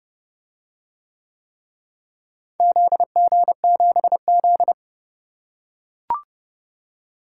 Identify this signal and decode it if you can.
Morse code, heard as a keyed tone, about 30 wpm, 700 Hz. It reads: ZG7Z